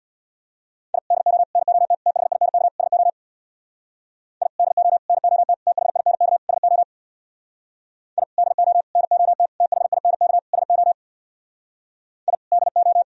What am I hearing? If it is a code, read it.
Morse code, heard as a keyed tone, about 64 wpm, 700 Hz. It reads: I DO NOT THINK SO I DO NOT THINK SO I DO NOT THINK SO I DO